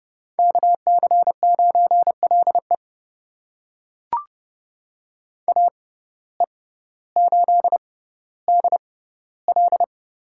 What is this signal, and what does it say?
Morse code, heard as a keyed tone, about 30 wpm, spaced as for 16 wpm, 700 Hz. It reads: KC9LE AE8DL
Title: USA Call Signs - 5 Characters - 6x Character Spacing - 30wpm